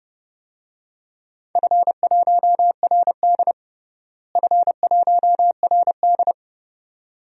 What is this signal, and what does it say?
Morse code, heard as a keyed tone, about 32 wpm, 700 Hz. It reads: F1RD F1RD
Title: DXCC-100 Call Signs - 4 Characters - Repeated 2x - 30wpm